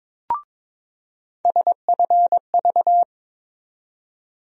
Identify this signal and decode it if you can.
Morse code, heard as a keyed tone, about 22 wpm, 700 Hz. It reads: SFV
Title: USA Call Sign Postfixes - 3 Letters 22wpm